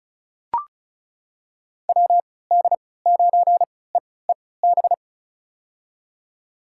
Morse code, heard as a keyed tone, about 36 wpm, spaced as for 20 wpm, 700 Hz. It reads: WD9EEB
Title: USA Call Signs - 6 Characters - 3x Character Spacing - 35wpm